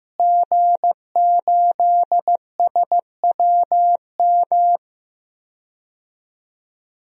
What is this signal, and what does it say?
Morse code, heard as a keyed tone, about 15 wpm, 700 Hz. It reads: G8SWM
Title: DXCC-100 Call Signs - 5 Characters - Repeated 3x - 15wpm